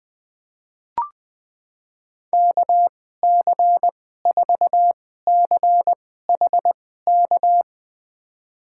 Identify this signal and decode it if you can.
Morse code, heard as a keyed tone, about 20 wpm, spaced as for 14 wpm, 700 Hz. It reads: KC4CHK
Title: USA Call Signs - 6 Characters - 2x Character Spacing - 20wpm